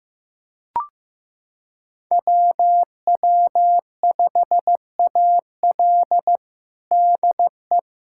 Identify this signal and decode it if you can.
Morse code, heard as a keyed tone, about 15 wpm, 700 Hz. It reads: WW5AL DE
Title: USA Callsign DE Callsign - 5 Characters - 15wpm